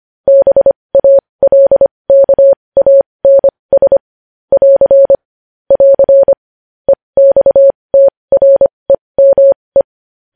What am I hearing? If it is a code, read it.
Morse code, heard as a keyed tone, about 25 wpm, spaced as for 20 wpm, 550 Hz. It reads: BALKANS <AR> <AR> EXTREME
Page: news-headlines-in-morse-code-at-20-wpm